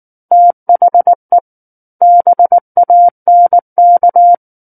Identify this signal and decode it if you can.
Morse code, heard as a keyed tone, about 19 wpm, 700 Hz. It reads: THE BANK
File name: Book One / 10-Chapter 10 - 19 WPM